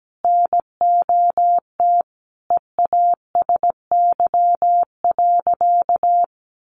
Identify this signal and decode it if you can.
Morse code, heard as a keyed tone, about 17 wpm, 700 Hz. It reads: NOT EASY.
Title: Sentences from Top 1500 Words 17wpm